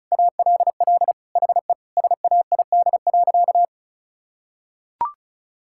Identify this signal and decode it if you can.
Morse code, heard as a keyed tone, about 36 wpm, 700 Hz. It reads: ALL HE SAID.
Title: Sentences from Top 1600 Words 35wpm